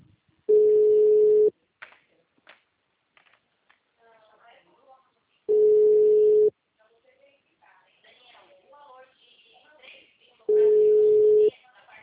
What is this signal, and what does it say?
Ringback tone (Europe, 425 Hz), with speech.